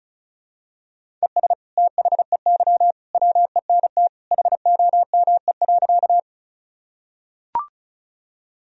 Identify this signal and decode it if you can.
Morse code, heard as a keyed tone, about 34 wpm, 700 Hz. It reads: ES THEY WENT HOME.
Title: Sentences from Top 200 Words and Abbreviations 35wpm